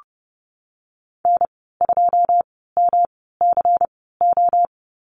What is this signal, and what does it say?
Morse code, heard as a keyed tone, about 30 wpm, spaced as for 17 wpm, 700 Hz. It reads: N2MCO
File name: Lesson 19g - Letter w - Call Signs 30wpm